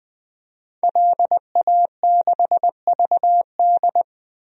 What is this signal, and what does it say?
Morse code, heard as a keyed tone, about 21 wpm, 700 Hz. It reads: LA6VD
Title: DXCC-100 Call Signs - 5 Characters - Repeated 3x - 20wpm